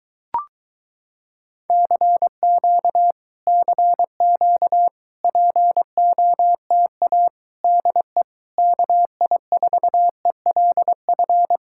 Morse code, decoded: CQ CQ POTA DE KI4ELF

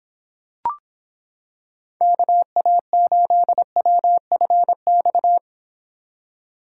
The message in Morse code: KA8WFX

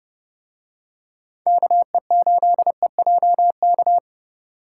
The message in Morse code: KE8EJK

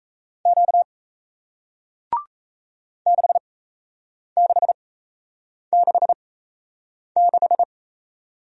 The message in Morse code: Q 6666

